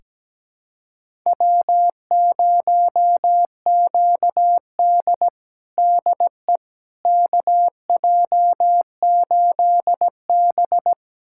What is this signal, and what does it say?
Morse code, heard as a keyed tone, about 17 wpm, 700 Hz. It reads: W0QD DE KJ8B
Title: USA Callsign DE Callsign - 4 Characters - 17wpm